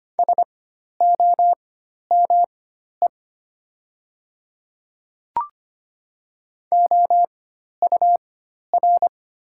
Morse code, heard as a keyed tone, about 25 wpm, spaced as for 12 wpm, 700 Hz. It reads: SOME OUR
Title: 4x Character Spacing - Top 200 Words - 25wpm